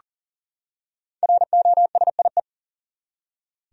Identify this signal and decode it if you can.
Morse code, heard as a keyed tone, about 40 wpm, 700 Hz. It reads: ROSIE
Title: USA Common Names - 40wpm